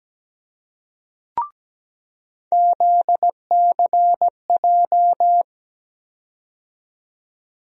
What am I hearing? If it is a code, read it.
Morse code, heard as a keyed tone, about 17 wpm, 700 Hz. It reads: ZCJ
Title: USA Call Sign Postfixes - 3 Letters 17wpm